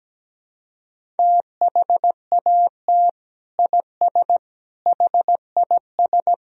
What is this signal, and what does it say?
Morse code, heard as a keyed tone, about 17 wpm, 700 Hz. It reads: THAT IS HIS
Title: Sentences from Top 1000 Words - Review Part 1 - 17wpm